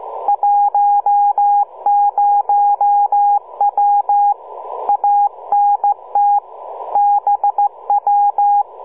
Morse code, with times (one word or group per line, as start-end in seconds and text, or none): 0.3-4.3 10W
4.9-6.4 ANT
7.0-8.6 BW